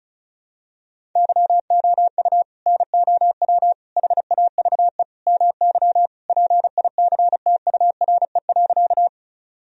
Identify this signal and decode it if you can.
Morse code, heard as a keyed tone, about 34 wpm, 700 Hz. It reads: YOU NOW HAVE MY PICTURE.
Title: Sentences from Top 1000 Words - Review Part 1 - 35wpm